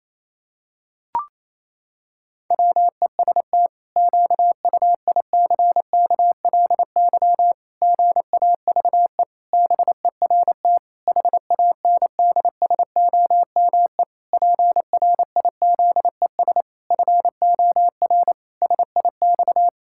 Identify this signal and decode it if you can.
Morse code, heard as a keyed tone, about 28 wpm, 700 Hz. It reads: WEST QUICKLY GAVE BERT HANDSOME PRIZES FOR SIX
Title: Pangram Challenge v2 - 28wpm